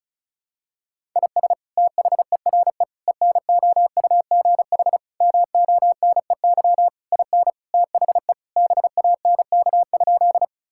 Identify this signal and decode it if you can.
Morse code, heard as a keyed tone, about 34 wpm, 700 Hz. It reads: IS THERE ENOUGH MONEY IN THE BANK?
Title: Sentences from Top 1000 Words - Review Part 6 - 35wpm